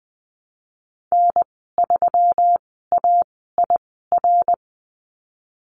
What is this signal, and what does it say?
Morse code, heard as a keyed tone, about 20 wpm, spaced as for 15 wpm, 700 Hz. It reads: N3AIR
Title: 147 Lesson 24g - Letter y - Call Signs 20wpm